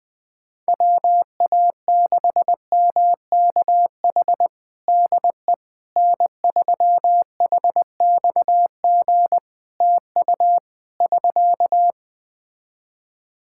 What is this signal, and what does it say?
Morse code, heard as a keyed tone, about 20 wpm, 700 Hz. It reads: WA6MKH DE N3HXG TU <SK>
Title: Practice QSO - Exchange 4 - Ending - 20wpm